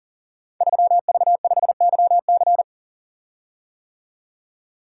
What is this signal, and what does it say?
Morse code, heard as a keyed tone, about 40 wpm, 700 Hz. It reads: 3V5YC
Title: Code Groups - 5 Letter:Numbers - 40wpm